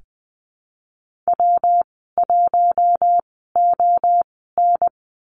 Morse code, heard as a keyed tone, about 20 wpm, spaced as for 14 wpm, 700 Hz. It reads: W1ON
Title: 122 Lesson 20g - Number 3 - Call Signs 20wpm